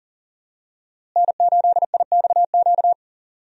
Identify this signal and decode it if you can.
Morse code, heard as a keyed tone, about 40 wpm, 700 Hz. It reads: N8IXQ
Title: USA Call Signs - 5 Characters - Repeated 2x - 40wpm